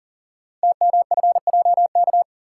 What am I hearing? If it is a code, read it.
Morse code, heard as a keyed tone, about 39 wpm, 700 Hz. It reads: TMFJK
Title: Code Groups - 5 Letter:Numbers - 40wpm